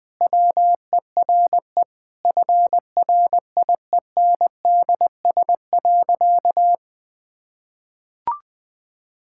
Morse code, heard as a keyed tone, about 20 wpm, 700 Hz. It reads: WERE FRIENDS.
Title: Sentences with 3-Word Phrases - 20wpm